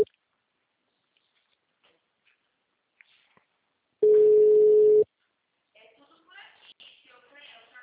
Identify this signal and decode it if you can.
Ringback tone (Europe, 425 Hz), with speech.
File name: out- -2008- - - .59639